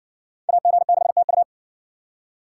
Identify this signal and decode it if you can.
Morse code, heard as a keyed tone, about 64 wpm, 700 Hz. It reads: AG6TU